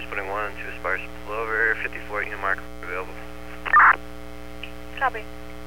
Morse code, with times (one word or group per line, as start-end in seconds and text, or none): none